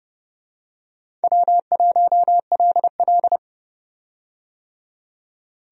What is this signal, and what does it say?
Morse code, heard as a keyed tone, about 31 wpm, 700 Hz. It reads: W1LL
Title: USA Call Signs - 4 Characters - 30wpm